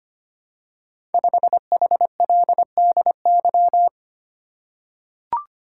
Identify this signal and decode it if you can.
Morse code, heard as a keyed tone, about 25 wpm, 700 Hz. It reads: 5HLDY
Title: Code Groups - 5 Letter:Numbers - 25wpm